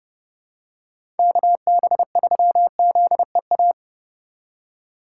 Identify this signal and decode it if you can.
Morse code, heard as a keyed tone, about 30 wpm, 700 Hz. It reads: KB3ZEA
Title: USA Call Signs - 6 Characters - 8x Character Spacing - 30wpm